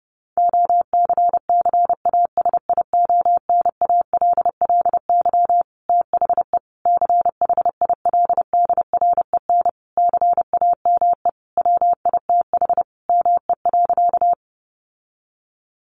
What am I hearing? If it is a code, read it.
Morse code, heard as a keyed tone, about 30 wpm, 700 Hz. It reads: OCCASIONALLY THE CHILDREN CAME WITH ME.
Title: Sentences from Top 1300 Words 30wpm